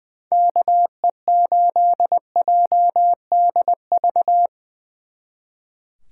Morse code, heard as a keyed tone, about 23 wpm, 700 Hz. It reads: KE8JDV
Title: Speed Racing - USA 2x3 Call Signs 20wpm